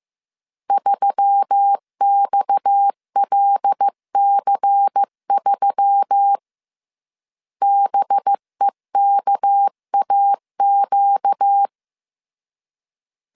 Morse code, heard as a keyed tone, about 15 wpm, 800 Hz. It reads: 3XLC3 BEKAQ
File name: CW / DET / 1 6 1500C S C 14wpm 2951